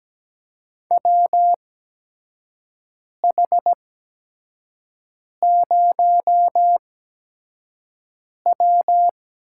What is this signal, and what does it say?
Morse code, heard as a keyed tone, about 17 wpm, spaced as for 5 wpm, 700 Hz. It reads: WH0W
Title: USA Call Signs - 8x Character Spacing - 17wpm